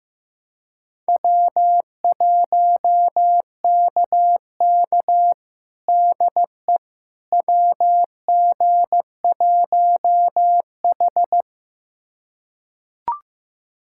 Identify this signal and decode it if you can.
Morse code, heard as a keyed tone, about 15 wpm, 700 Hz. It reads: W1KK DE WG1H
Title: USA Callsign DE Callsign - 4 Characters - 15wpm